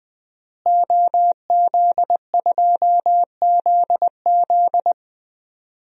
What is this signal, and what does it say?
Morse code, heard as a keyed tone, about 20 wpm, 700 Hz. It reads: OZ2ZZ
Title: DXCC-100 Call Signs - 5 Characters - Repeated 2x - 20wpm